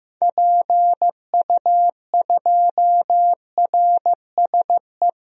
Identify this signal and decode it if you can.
Morse code, heard as a keyed tone, about 15 wpm, 700 Hz. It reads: PU2RSE
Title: DXCC-100 Call Signs - 6 Characters - 15wpm